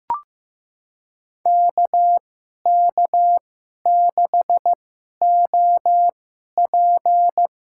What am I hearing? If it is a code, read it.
Morse code, heard as a keyed tone, about 15 wpm, spaced as for 11 wpm, 700 Hz. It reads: KK6OP
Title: USA Call Signs - 2x Character Spacing - 15wpm